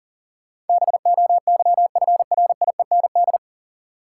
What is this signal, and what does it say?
Morse code, heard as a keyed tone, about 40 wpm, 700 Hz. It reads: BOYFRIEND
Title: Sentences from Top 1000 Words - Review Part 5 - New Word Drill - 40wpm